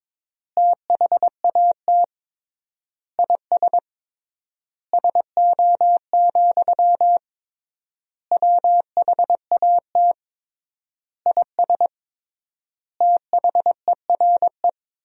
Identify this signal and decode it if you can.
Morse code, heard as a keyed tone, about 22 wpm, 700 Hz. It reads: THAT IS SO, WHAT IS THERE
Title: 3x Word-Spacing - Sentences from Top 200 Words - 22wpm